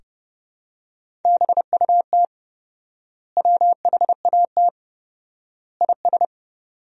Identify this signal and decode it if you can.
Morse code, heard as a keyed tone, about 30 wpm, 700 Hz. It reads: BUT WHAT IS
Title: 4x Word-Spacing - Sentences from Top 100 Words - 30wpm